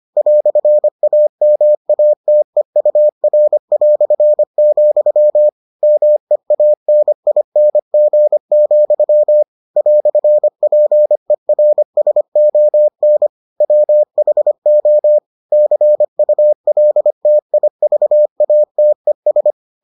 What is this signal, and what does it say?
Morse code, heard as a keyed tone, about 25 wpm, 600 Hz. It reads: "AMATEUR", MEANING, "PERSON WHO CULTIVATES